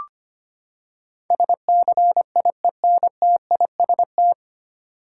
Morse code, heard as a keyed tone, about 25 wpm, 700 Hz. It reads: SCIENTIST
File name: Nine Letter Words - 25wpm